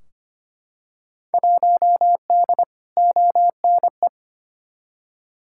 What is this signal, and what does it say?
Morse code, heard as a keyed tone, about 25 wpm, 700 Hz. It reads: 1D ONE
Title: Field Day Exchanges - 25wpm